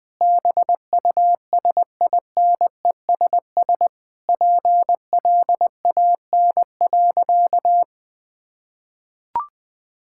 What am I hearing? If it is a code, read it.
Morse code, heard as a keyed tone, about 21 wpm, 700 Hz. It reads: BUSINESS PLAN.
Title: Sentences from Top 1000 Words - Review Part 6 - 20wpm